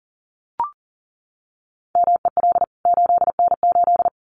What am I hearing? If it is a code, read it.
Morse code, heard as a keyed tone, about 40 wpm, 700 Hz. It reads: MEL 8N8
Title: CWT Exchange 40wpm